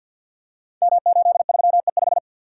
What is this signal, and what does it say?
Morse code, heard as a keyed tone, about 46 wpm, 700 Hz. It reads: M83E5
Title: Code Groups - 5 Letter:Numbers - 50wpm